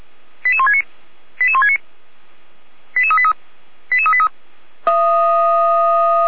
Five-tone selective call (ZVEI), heard as quick runs of five tones, tones of about 70 ms. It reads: 88169 88169 88383 88383
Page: /455/SDR/ZVEI/